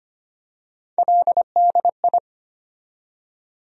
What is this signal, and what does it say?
Morse code, heard as a keyed tone, about 25 wpm, 700 Hz. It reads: LDI
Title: USA Call Sign Postfixes - 3 Letters 25wpm